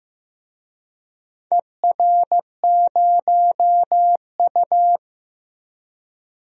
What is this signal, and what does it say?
Morse code, heard as a keyed tone, about 15 wpm, 700 Hz. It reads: ER0U
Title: DXCC-100 Call Signs - 4 Characters - Repeated 2x - 15wpm